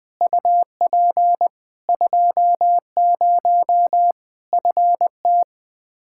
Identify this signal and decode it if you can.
Morse code, heard as a keyed tone, about 20 wpm, 700 Hz. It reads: UP 20 FT
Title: Practice QSO - Exchange 2 - Info - 20wpm